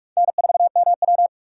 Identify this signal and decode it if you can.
Morse code, heard as a keyed tone, about 44 wpm, 700 Hz. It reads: N4MW